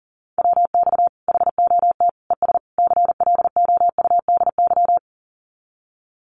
Morse code, heard as a keyed tone, about 40 wpm, 700 Hz. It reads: WX HOT ES CLOUDY
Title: Common QSO Phrases 40wpm